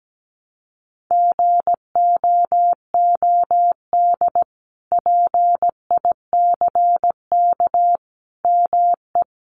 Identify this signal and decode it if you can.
Morse code, heard as a keyed tone, about 17 wpm, 700 Hz. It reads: GOOD PICK ME
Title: Sentences from Top 1000 Words 17wpm